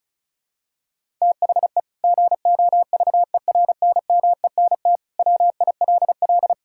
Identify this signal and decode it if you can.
Morse code, heard as a keyed tone, about 36 wpm, 700 Hz. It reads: THE GOVERNMENT WILL
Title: Sentences from Top 1000 Words - Review Part 5 - 35wpm